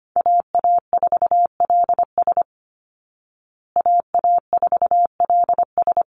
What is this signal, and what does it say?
Morse code, heard as a keyed tone, about 25 wpm, 700 Hz. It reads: AA4LS AA4LS
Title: Call Signs Repeated 2x - 25wpm